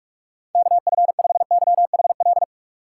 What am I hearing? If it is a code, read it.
Morse code, heard as a keyed tone, about 46 wpm, 700 Hz. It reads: KF5YHL
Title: USA Call Signs - 6 Characters - 3x Character Spacing - 45wpm